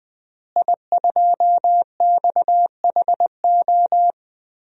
Morse code, heard as a keyed tone, about 21 wpm, 700 Hz. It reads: I2XHO
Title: DXCC-100 Call Signs - 5 Characters - Repeated 3x - 20wpm